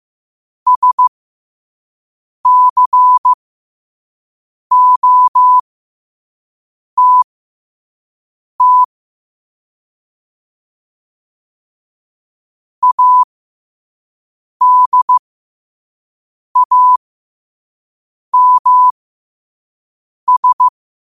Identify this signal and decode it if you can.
Morse code, heard as a keyed tone, about 15 wpm, spaced as for 5 wpm, 1000 Hz. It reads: SCOTT ADAMS
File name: qotdfiles / 5wpm